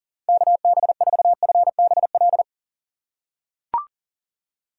Morse code, heard as a keyed tone, about 40 wpm, 700 Hz. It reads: KB4FBL
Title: USA Call Signs - 6 Characters - 2x Character Spacing - 40wpm